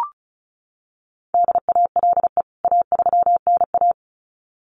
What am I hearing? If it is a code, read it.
Morse code, heard as a keyed tone, about 34 wpm, 700 Hz. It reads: DALE A3NA
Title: CWT Exchange 35wpm